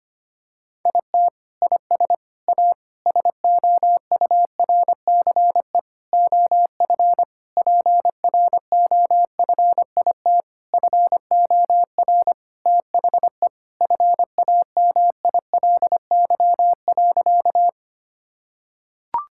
Morse code, heard as a keyed tone, about 25 wpm, 700 Hz. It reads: IT IS A SOURCE OF PROFIT FOR THE FAMILY.
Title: Sentences from Top 1000 Words - Review Part 5 - 25wpm